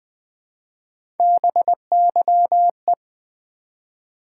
Morse code, heard as a keyed tone, about 20 wpm, 700 Hz. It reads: BYE
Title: Build A Word - 1 Letter At A Time - 3-Letter Words - 20wpm